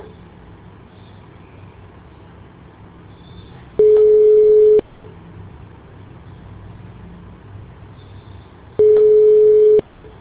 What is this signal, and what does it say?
Ringback tone (Europe, 425 Hz).